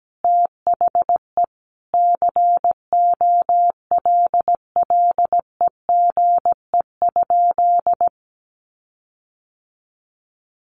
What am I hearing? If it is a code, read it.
Morse code, heard as a keyed tone, about 17 wpm, 700 Hz. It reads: THE COLLEGE?
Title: Sentences from Top 1300 Words 17wpm